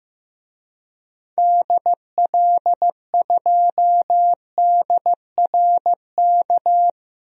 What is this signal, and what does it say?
Morse code, heard as a keyed tone, about 15 wpm, 700 Hz. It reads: DL2DRK
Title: DXCC-100 Call Signs - 6 Characters - Repeated 2x - 15wpm